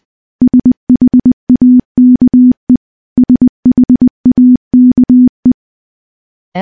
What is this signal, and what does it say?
Morse code, heard as a keyed tone, about 20 wpm, 250 Hz. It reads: SHAKE SHAKE
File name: morse-training-20WPM